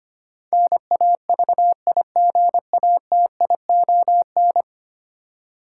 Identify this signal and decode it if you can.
Morse code, heard as a keyed tone, about 25 wpm, 700 Hz. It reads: NAVIGATION
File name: Ten Letter Words - 25wpm